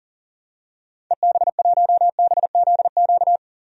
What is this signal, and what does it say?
Morse code, heard as a keyed tone, about 40 wpm, 700 Hz. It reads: EB1BZQ
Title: DXCC-100 Call Signs - 6 Characters - 40wpm